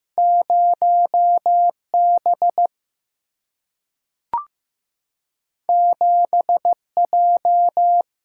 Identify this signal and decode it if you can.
Morse code, heard as a keyed tone, about 15 wpm, 700 Hz. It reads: 0 B 7 J
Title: Single-Letter Single-Number 15wpm